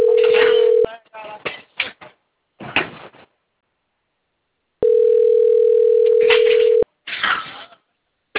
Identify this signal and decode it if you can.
Ringback tone (North America), 440 + 480 Hz.
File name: rg-1600-unknown- - - .18280